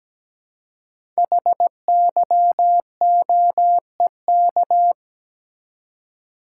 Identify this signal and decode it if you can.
Morse code, heard as a keyed tone, about 17 wpm, 700 Hz. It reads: HYOEK
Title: Code Groups - 5 Letters - 17wpm